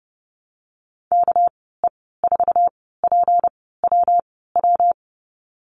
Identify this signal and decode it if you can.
Morse code, heard as a keyed tone, about 30 wpm, spaced as for 17 wpm, 700 Hz. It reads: KE4PWW
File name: Lesson 37g - Letter x - Call Signs 30wpm